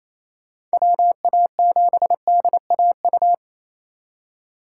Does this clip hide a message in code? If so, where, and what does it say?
Yes: Morse code from 0.7 to 3.3 s, heard as a keyed tone, about 36 wpm, 700 Hz. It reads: WA7DAU